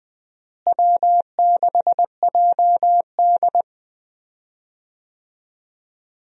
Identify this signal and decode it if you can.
Morse code, heard as a keyed tone, about 21 wpm, 700 Hz. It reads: W6JD
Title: USA Call Signs - 4 Characters - Repeated 3x - 20wpm